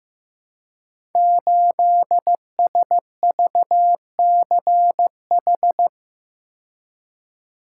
Morse code, heard as a keyed tone, about 15 wpm, 700 Hz. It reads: 8SVCH
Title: Code Groups - 5 Letter:Numbers - 15wpm